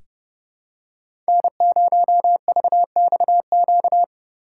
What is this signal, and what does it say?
Morse code, heard as a keyed tone, about 30 wpm, 700 Hz. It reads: N0VXQ
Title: USA Call Signs - 5 Characters - 4x Character Spacing - 30wpm